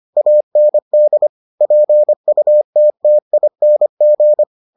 Morse code, heard as a keyed tone, about 25 wpm, 600 Hz. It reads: AND PUTTING